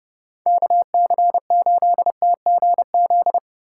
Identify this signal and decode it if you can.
Morse code, heard as a keyed tone, about 32 wpm, 700 Hz. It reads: KC8TGZ